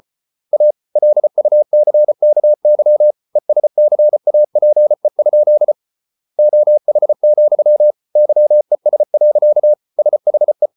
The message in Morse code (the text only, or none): A LUCKY ESCAPE? OH, YES. SHE